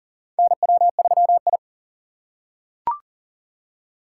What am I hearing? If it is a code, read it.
Morse code, heard as a keyed tone, about 40 wpm, 700 Hz. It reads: NW3I
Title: USA Call Signs - 4x Character Spacing - 40wpm